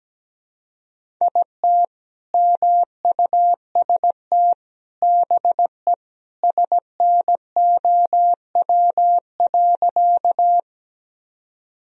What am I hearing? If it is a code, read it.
Morse code, heard as a keyed tone, about 17 wpm, 700 Hz. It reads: IT MUST BE SNOW.
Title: Sentences from Top 1000 Words - Review Part 3 - 17wpm